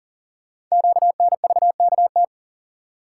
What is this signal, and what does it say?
Morse code, heard as a keyed tone, about 39 wpm, 700 Hz. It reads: QNVKT
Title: Code Groups - 5 Letter:Numbers - 40wpm